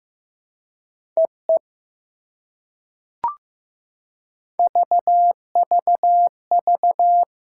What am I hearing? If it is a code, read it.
Morse code, heard as a keyed tone, about 15 wpm, 690 Hz. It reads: EE VVV
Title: SOTA QSO Practice 15wpm